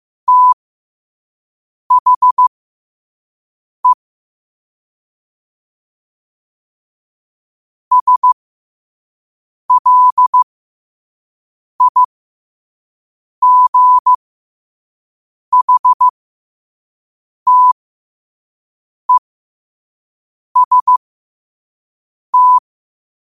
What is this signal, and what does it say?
Morse code, heard as a keyed tone, about 15 wpm, spaced as for 5 wpm, 1000 Hz. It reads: THE SLIGHTEST
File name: qotdfiles / 5wpm